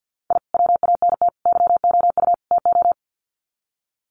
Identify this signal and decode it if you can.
Morse code, heard as a keyed tone, about 50 wpm, 700 Hz. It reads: I WANT YOU TO
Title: Sets of 4 Words 50wpm